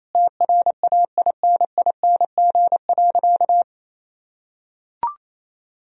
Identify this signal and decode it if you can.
Morse code, heard as a keyed tone, about 28 wpm, 700 Hz. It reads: TRAINING.
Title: Sentences from Top 1000 Words - Review Part 4 - 28wpm